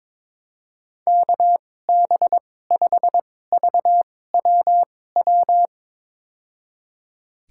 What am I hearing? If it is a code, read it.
Morse code, heard as a keyed tone, about 22 wpm, spaced as for 16 wpm, 700 Hz. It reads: KB5VWW
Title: USA Call Signs - 2x Character Spacing - 22wpm